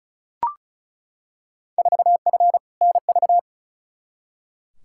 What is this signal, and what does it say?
Morse code, heard as a keyed tone, about 36 wpm, 700 Hz. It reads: 4F NV
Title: Field Day Exchanges Encore - 35wpm